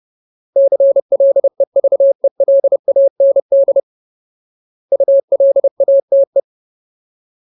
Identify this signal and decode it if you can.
Morse code, heard as a keyed tone, about 30 wpm, 550 Hz. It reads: CLEVELAND ULATE
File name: Ultimate Word List / 30WPM Ultimate